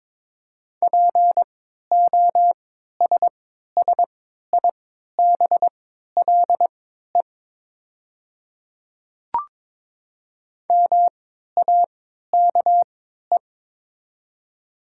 Morse code, heard as a keyed tone, about 22 wpm, spaced as for 12 wpm, 700 Hz. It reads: POSSIBLE MAKE
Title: 3x Character Spacing - Top 500 Words - 22wpm